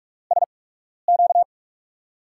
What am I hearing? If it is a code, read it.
Morse code, heard as a keyed tone, about 44 wpm, spaced as for 12 wpm, 700 Hz. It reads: SQ
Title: USA Call Signs - 6 Characters - 8x Character Spacing - 45wpm